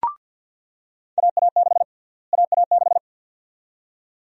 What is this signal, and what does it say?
Morse code, heard as a keyed tone, about 46 wpm, 700 Hz. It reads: AA6 AA6
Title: USA Call Sign Prefixes - Repeated 2x - 50wpm